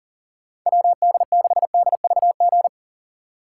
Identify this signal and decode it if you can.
Morse code, heard as a keyed tone, about 37 wpm, 700 Hz. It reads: WD6DVG
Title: USA Call Signs - 6 Characters - 2x Character Spacing - 40wpm